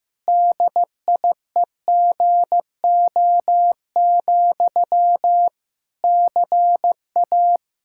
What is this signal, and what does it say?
Morse code, heard as a keyed tone, about 16 wpm, 700 Hz. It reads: DIEGO, CA
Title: Practice QSO - Exchange 1 - Beginning - 15wpm